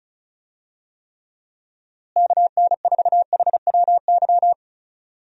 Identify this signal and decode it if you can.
Morse code, heard as a keyed tone, about 36 wpm, 700 Hz. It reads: KN4HWY
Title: USA Call Signs - 6 Characters - Repeated 3x - 35wpm